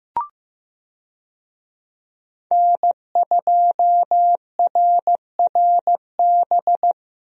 Morse code, heard as a keyed tone, about 15 wpm, 700 Hz. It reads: N2RRB
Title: USA Call Signs - 5 Characters - Repeated 3x - 15wpm